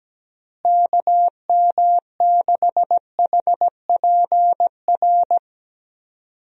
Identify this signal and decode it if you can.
Morse code, heard as a keyed tone, about 17 wpm, 700 Hz. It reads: KM6HPR
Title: USA Call Signs - 6 Characters - Repeated 3x - 17wpm